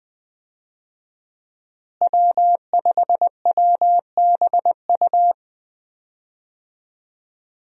Morse code, heard as a keyed tone, about 20 wpm, 700 Hz. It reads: W5WBU
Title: USA Call Signs - 5 Characters - Repeated 2x - 20wpm